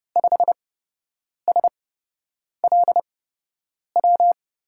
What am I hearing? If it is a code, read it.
Morse code, heard as a keyed tone, about 30 wpm, spaced as for 8 wpm, 700 Hz. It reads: 5SLW
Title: USA Call Signs - 6 Characters - 8x Character Spacing - 30wpm